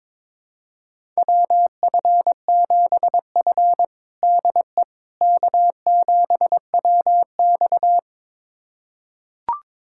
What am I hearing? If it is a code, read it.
Morse code, heard as a keyed tone, about 22 wpm, 700 Hz. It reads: WF7F DE K7WX
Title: USA Callsign DE Callsign - 4 Characters - 22wpm